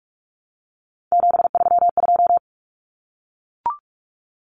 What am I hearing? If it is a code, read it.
Morse code, heard as a keyed tone, about 44 wpm, 700 Hz. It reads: 732
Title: Common Cut Numbers - 3-Digit 45wpm